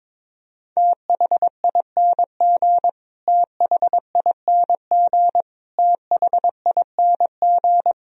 Morse code, heard as a keyed tone, about 22 wpm, 700 Hz. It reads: THING THING THING
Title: Top 100 Words Repeated 22wpm